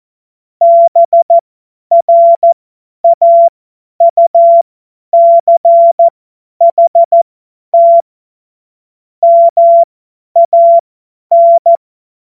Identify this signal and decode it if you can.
Morse code, heard as a keyed tone, about 14 wpm, spaced as for 10 wpm, 680 Hz. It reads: BRAUCHT MAN
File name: Morsekurs / Witze / Witz 13. April 2021, 50 F70